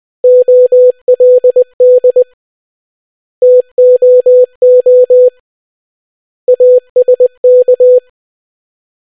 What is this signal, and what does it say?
Morse code, heard as a keyed tone, about 20 wpm, 500 Hz. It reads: OLD TOO ASK